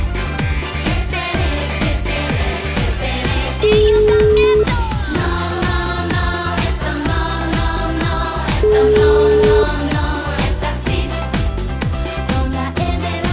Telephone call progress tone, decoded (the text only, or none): ringback tone